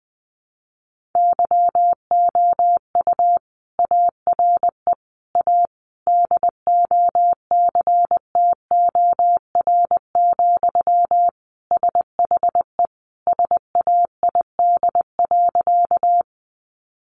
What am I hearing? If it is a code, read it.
Morse code, heard as a keyed tone, about 20 wpm, 700 Hz. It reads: YOU ARE A DOCTOR, SHE SAID.
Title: Sentences from Top 1500 Words 20wpm